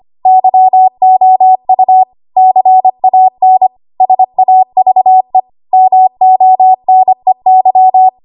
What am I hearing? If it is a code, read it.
Morse code, heard as a keyed tone, about 25 wpm, 750 Hz. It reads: YOU CAN SAVE MONEY
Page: W1AW Web Code Practice Files